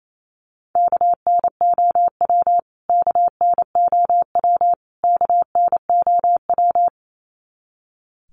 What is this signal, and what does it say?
Morse code, heard as a keyed tone, about 28 wpm, 700 Hz. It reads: KNOW KNOW KNOW
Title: Top 300 Words Repeated 28wpm